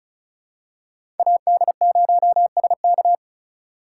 Morse code, heard as a keyed tone, about 36 wpm, 700 Hz. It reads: AD0SK
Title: USA Call Signs - 5 Characters - 4x Character Spacing - 35wpm